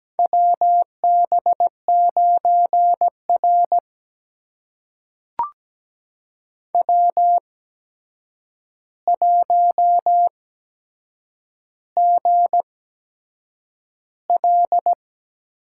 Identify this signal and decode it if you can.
Morse code, heard as a keyed tone, about 17 wpm, spaced as for 7 wpm, 700 Hz. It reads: WB9R W1GL
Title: USA Call Signs - 4 Characters - 8x Character Spacing - 17wpm